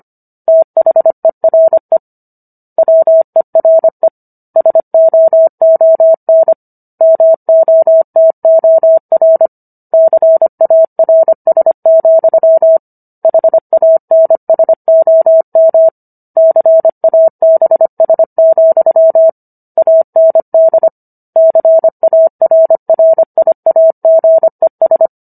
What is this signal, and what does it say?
Morse code, heard as a keyed tone, about 25 wpm, 650 Hz. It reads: THERE WERE SOON MOTOR CARS, HANSOM CABS, AND CARRIAGES